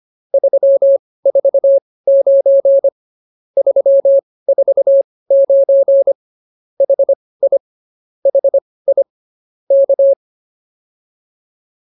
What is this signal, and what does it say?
Morse code, heard as a keyed tone, about 24 wpm, spaced as for 18 wpm, 550 Hz. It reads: 349 349 HI HI K
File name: POTA208 18